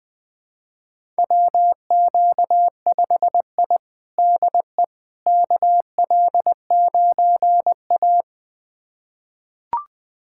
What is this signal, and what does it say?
Morse code, heard as a keyed tone, about 20 wpm, 700 Hz. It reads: WQ5I DE KL9A
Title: USA Callsign DE Callsign - 4 Characters - 20wpm